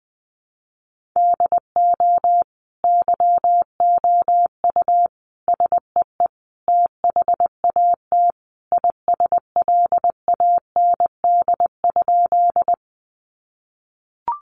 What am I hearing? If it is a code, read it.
Morse code, heard as a keyed tone, about 20 wpm, 700 Hz. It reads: DO YOU SEE THAT ISLAND?
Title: Sentences from Top 500 Words 20wpm